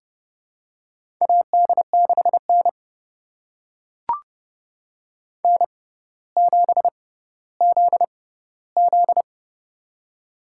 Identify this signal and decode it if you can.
Morse code, heard as a keyed tone, about 30 wpm, spaced as for 15 wpm, 700 Hz. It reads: AD6N N7ZZ
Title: USA Call Signs - 4 Characters - 6x Character Spacing - 30wpm